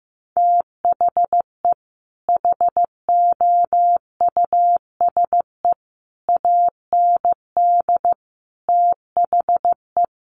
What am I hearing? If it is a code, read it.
Morse code, heard as a keyed tone, about 15 wpm, 700 Hz. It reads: THE HOUSE AND THE
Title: Sets of 5 Words Encore2 15wpm